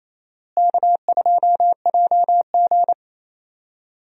Morse code, heard as a keyed tone, about 28 wpm, 700 Hz. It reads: K2JG
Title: USA Call Signs - 4 Characters - Repeated 2x - 28wpm